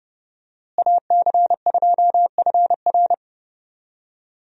Morse code, heard as a keyed tone, about 30 wpm, 700 Hz. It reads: AC2FR
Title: USA Call Signs - 5 Characters - Repeated 3x - 30wpm